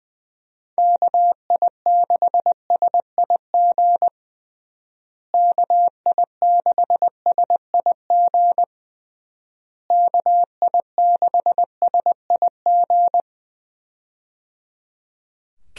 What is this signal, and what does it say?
Morse code, heard as a keyed tone, about 20 wpm, 700 Hz. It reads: KI6SIG KI6SIG KI6SIG
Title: USA Call Signs - 6 Characters - Repeated 3x - 20wpm